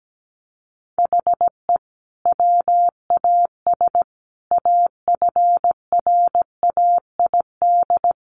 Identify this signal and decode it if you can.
Morse code, heard as a keyed tone, about 17 wpm, 700 Hz. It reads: HE WAS AFRAID
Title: Sentences from Top 1500 Words 17wpm